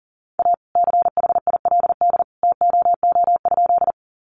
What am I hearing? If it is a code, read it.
Morse code, heard as a keyed tone, about 40 wpm, 690 Hz. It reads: A CHILD TOO?
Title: Sentences from Top 800 Words 40wpm